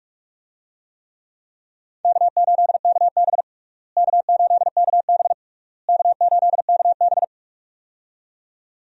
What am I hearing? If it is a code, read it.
Morse code, heard as a keyed tone, about 46 wpm, 700 Hz. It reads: K8KB K8KB K8KB
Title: USA Call Signs - 4 Characters - Repeated 3x - 45wpm